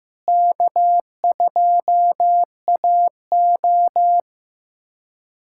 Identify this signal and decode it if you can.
Morse code, heard as a keyed tone, about 15 wpm, 700 Hz. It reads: K2AO
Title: USA Callsign DE Callsign - 4 Characters - 15wpm